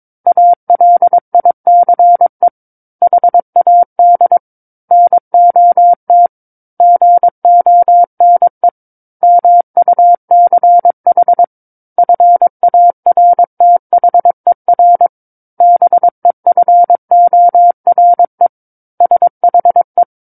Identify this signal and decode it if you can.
Morse code, heard as a keyed tone, about 22 wpm, 700 Hz. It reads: ALICE HAD NOT GONE MUCH FARTHER BEFORE SHE
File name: 06-Chapter-06 22 WPM